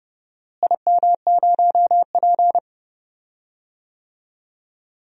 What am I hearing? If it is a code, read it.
Morse code, heard as a keyed tone, about 30 wpm, 700 Hz. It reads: IM0P
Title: DXCC-100 Call Signs - 4 Characters - Repeated 3x - 30wpm